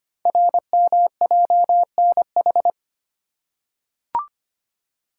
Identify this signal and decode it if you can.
Morse code, heard as a keyed tone, about 25 wpm, 700 Hz. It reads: RMJNH
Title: Code Groups - 5 Letters - 25wpm